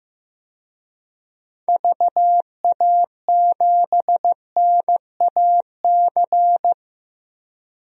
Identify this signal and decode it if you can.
Morse code, heard as a keyed tone, about 15 wpm, 700 Hz. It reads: VA7NAC
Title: DXCC-100 Call Signs - 6 Characters - 15wpm